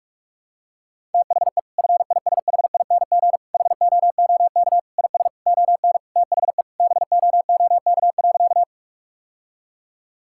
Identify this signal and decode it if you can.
Morse code, heard as a keyed tone, about 46 wpm, 700 Hz. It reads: THE FISHING HOOK IS ON THE BOOK.
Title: Sentences from Top 1600 Words 45wpm